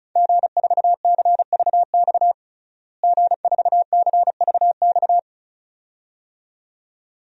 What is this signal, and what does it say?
Morse code, heard as a keyed tone, about 36 wpm, 700 Hz. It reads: G4CVX G4CVX
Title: DXCC-100 Call Signs - 5 Characters - Repeated 2x - 35wpm